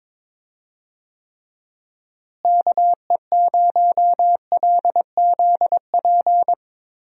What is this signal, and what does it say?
Morse code, heard as a keyed tone, about 21 wpm, 700 Hz. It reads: KE0LZP
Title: USA Call Signs - 6 Characters - Repeated 3x - 22wpm